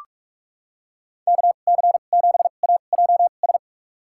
Morse code, heard as a keyed tone, about 46 wpm, spaced as for 33 wpm, 700 Hz. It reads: KC7AJS
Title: USA Call Signs - 6 Characters - 2x Character Spacing - 45wpm